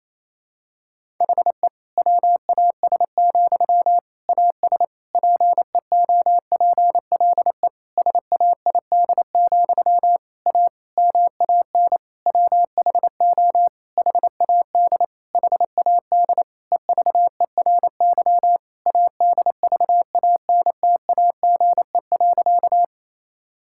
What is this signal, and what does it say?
Morse code, heard as a keyed tone, about 28 wpm, 700 Hz. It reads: HE WAS, AS PEOPLE SAID, A MAN WHO HAD HAD EVERY ADVANTAGE.
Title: Sentences from Top 1000 Words - Review Part 6 - 28wpm